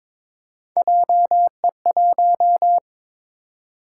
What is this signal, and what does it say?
Morse code, heard as a keyed tone, about 22 wpm, 700 Hz. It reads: JE1
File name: practice / 22wpm / Three Letter-Number Combinations - 22wpm